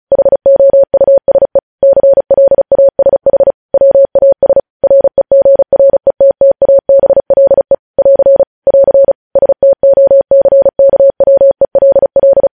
Morse code, heard as a keyed tone, about 36 wpm, 550 Hz. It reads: HOUSE CLASH WAS REGRETTABLE <AR> <AR> STOCKWELL
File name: podcast / News headlines-1x-35x35